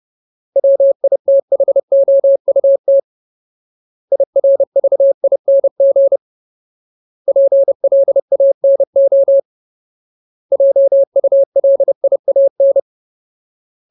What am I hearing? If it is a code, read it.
Morse code, heard as a keyed tone, about 30 wpm, 550 Hz. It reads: WITHOUT IRVING PLANO JULIAN
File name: Ultimate Word List / 30WPM Ultimate